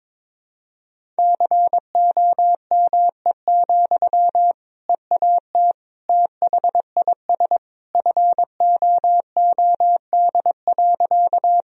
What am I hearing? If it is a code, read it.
Morse code, heard as a keyed tone, about 22 wpm, 700 Hz. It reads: COME, EAT THIS FOOD.
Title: Sentences from Top 1000 Words - Review Part 1 - 22wpm